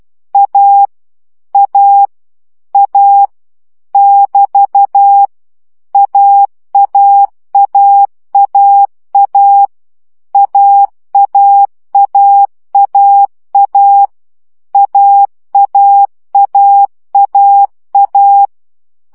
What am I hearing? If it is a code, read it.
Morse code, heard as a keyed tone, about 12 wpm, 800 Hz. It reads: A A A <BT> AAAAA AAAAA AAAAA
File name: CW gyakorlas a 10 baud